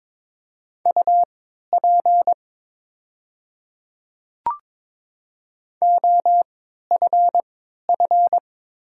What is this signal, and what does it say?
Morse code, heard as a keyed tone, about 22 wpm, spaced as for 12 wpm, 700 Hz. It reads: UP OFF
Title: 3x Character Spacing - Top 200 Words - 22wpm